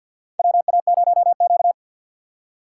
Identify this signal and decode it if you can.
Morse code, heard as a keyed tone, about 53 wpm, 700 Hz. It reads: WA0Q
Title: USA Call Signs - 3x Character Spacing - 50wpm